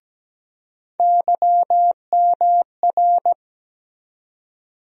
Morse code, heard as a keyed tone, about 17 wpm, 700 Hz. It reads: YMR
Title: USA Call Sign Postfixes - 3 Letters 17wpm